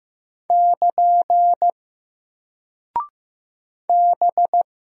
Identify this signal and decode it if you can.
Morse code, heard as a keyed tone, about 15 wpm, 700 Hz. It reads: <KN> B
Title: Single Character (Letter:Number:Punctuation:Prosign) Alt 15wpm